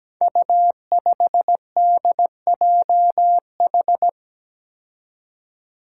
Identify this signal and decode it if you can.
Morse code, heard as a keyed tone, about 17 wpm, 700 Hz. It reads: U5DJH
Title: Five Challenging Letter-Number Combinations 17wpm